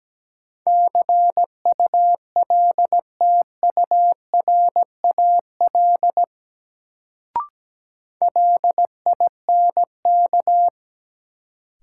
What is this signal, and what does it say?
Morse code, heard as a keyed tone, about 17 wpm, 700 Hz. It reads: CULTURAL LINK
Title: Sentences from Top 1500 Words - New Word Drill Alt - 17wpm